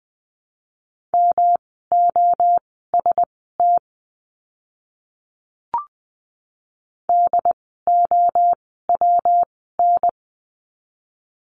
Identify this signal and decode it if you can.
Morse code, heard as a keyed tone, about 20 wpm, spaced as for 14 wpm, 700 Hz. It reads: MOST DOWN
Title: 237 Lesson 38f - Letter q - Words 20wpm